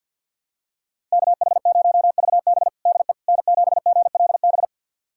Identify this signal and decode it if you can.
Morse code, heard as a keyed tone, about 51 wpm, 700 Hz. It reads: KH0VB DE N7GLB
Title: USA Callsign DE Callsign - 5 Characters - 50wpm